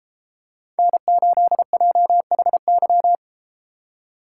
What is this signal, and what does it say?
Morse code, heard as a keyed tone, about 30 wpm, 700 Hz. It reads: N8JHY